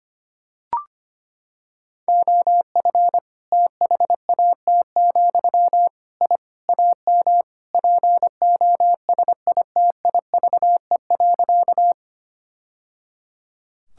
Morse code, heard as a keyed tone, about 25 wpm, 700 Hz. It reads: OF THAT, I AM POSITIVE.
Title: Sentences from Top 1000 Words - Review Part 5 - 25wpm